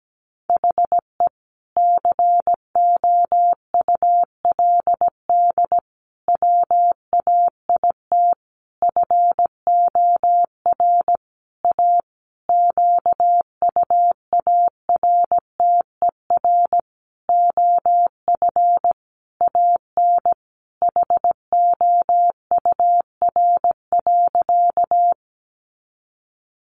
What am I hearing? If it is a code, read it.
Morse code, heard as a keyed tone, about 17 wpm, 700 Hz. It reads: HE COULD WAIT FOR A QUARTER OF AN HOUR.
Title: Sentences from Top 1300 Words 17wpm